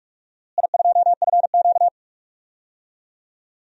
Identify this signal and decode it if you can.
Morse code, heard as a keyed tone, about 45 wpm, 700 Hz. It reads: I2FQ
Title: DXCC-100 Call Signs - 4 Characters - Repeated 3x - 45wpm